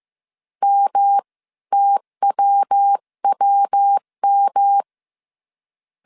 Morse code, heard as a keyed tone, about 15 wpm, spaced as for 13 wpm, 800 Hz. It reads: MTWWM